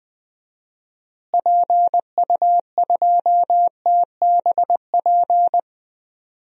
Morse code, heard as a keyed tone, about 20 wpm, 700 Hz. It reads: PU2TBP
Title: DXCC-100 Call Signs - 6 Characters - Repeated 2x - 20wpm